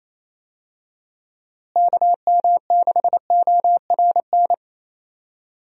Morse code, heard as a keyed tone, about 28 wpm, 700 Hz. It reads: KM6ORN
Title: USA Call Signs - 6 Characters - Repeated 2x - 28wpm